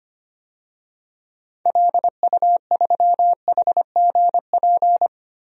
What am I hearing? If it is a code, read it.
Morse code, heard as a keyed tone, about 25 wpm, 700 Hz. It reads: LU3HGP
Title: DXCC-100 Call Signs - 6 Characters - 25wpm